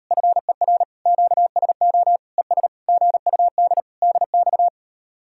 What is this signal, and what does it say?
Morse code, heard as a keyed tone, about 38 wpm, 700 Hz. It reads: FER QSO ES GUD DX